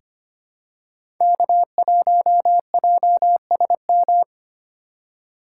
Morse code, heard as a keyed tone, about 25 wpm, 700 Hz. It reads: K1JSM
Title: USA Call Signs - 3x Character Spacing - 25wpm